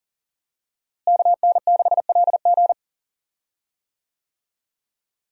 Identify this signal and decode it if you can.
Morse code, heard as a keyed tone, about 40 wpm, 700 Hz. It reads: KN6LG
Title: USA Call Signs - 5 Characters - Repeated 2x - 40wpm